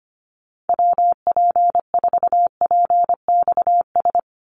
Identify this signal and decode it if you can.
Morse code, heard as a keyed tone, about 25 wpm, 700 Hz. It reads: WP4PXS
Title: USA Call Signs 25wpm